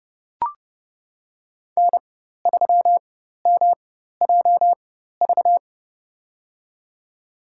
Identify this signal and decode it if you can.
Morse code, heard as a keyed tone, about 30 wpm, spaced as for 14 wpm, 700 Hz. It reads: N3MJV
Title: USA Call Signs - 5 Characters - 4x Character Spacing - 30wpm